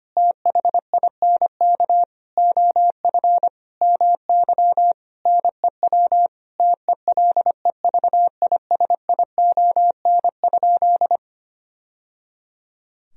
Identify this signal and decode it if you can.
Morse code, heard as a keyed tone, about 25 wpm, 700 Hz. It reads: THINK OF MY NEW TELEVISION?
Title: Sentences from Top 1000 Words - Review Part 4 - 25wpm